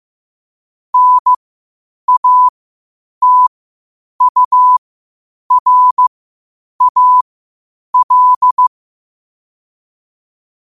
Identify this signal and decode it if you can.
Morse code, heard as a keyed tone, about 15 wpm, spaced as for 8 wpm, 1000 Hz. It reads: NATURAL
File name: qotdfiles / 8wpm